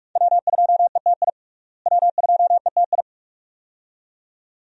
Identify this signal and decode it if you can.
Morse code, heard as a keyed tone, about 41 wpm, 700 Hz. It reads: W2ETI W2ETI
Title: USA Call Signs - 5 Characters - Repeated 2x - 45wpm